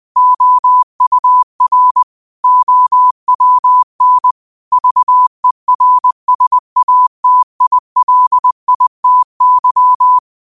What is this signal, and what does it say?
Morse code, heard as a keyed tone, about 20 wpm, 1000 Hz. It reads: OUR OWN VERSATILITY